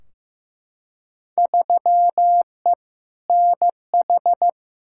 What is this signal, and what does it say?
Morse code, heard as a keyed tone, about 15 wpm, 700 Hz. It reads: 3E NH
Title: Field Day Exchanges Encore - 15wpm